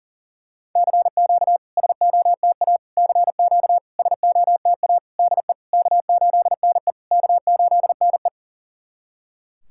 Morse code, heard as a keyed tone, about 40 wpm, 690 Hz. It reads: CQ SOTA CQ SOTA DE K8NE K8NE
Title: SOTA QSO Practice 40wpm